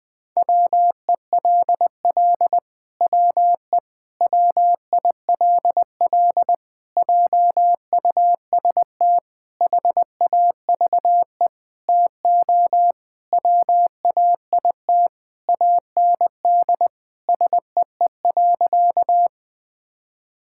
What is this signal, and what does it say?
Morse code, heard as a keyed tone, about 20 wpm, 700 Hz. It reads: WELL WE WILL JUST HAVE TO WAIT AND SEE.
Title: Sentences with 7-Word Phrases - 20wpm